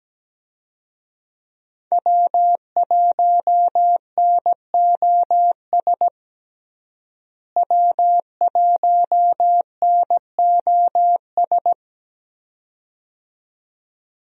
Morse code, heard as a keyed tone, about 17 wpm, 700 Hz. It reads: W1NOS W1NOS
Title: USA Call Signs - 5 Characters - Repeated 2x - 17wpm